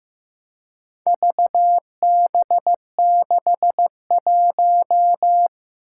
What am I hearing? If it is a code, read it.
Morse code, heard as a keyed tone, about 15 wpm, 700 Hz. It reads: VB61
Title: Four Challenging Letter-Number Combinations 15wpm